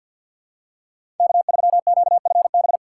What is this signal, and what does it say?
Morse code, heard as a keyed tone, about 47 wpm, 700 Hz. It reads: K3QFB
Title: USA Call Signs - 5 Characters - 2x Character Spacing - 50wpm